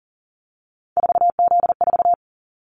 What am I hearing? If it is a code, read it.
Morse code, heard as a keyed tone, about 40 wpm, 700 Hz. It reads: 4Z4
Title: Callsign Prefixes for Top 50 DXCC - 40wpm